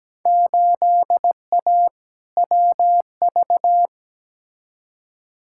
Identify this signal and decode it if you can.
Morse code, heard as a keyed tone, about 17 wpm, 700 Hz. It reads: 8A WV
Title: Field Day Exchanges Encore - 17wpm